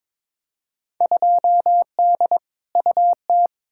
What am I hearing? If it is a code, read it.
Morse code, heard as a keyed tone, about 21 wpm, 700 Hz. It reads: 2D UT
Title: Field Day Exchanges Encore - 22wpm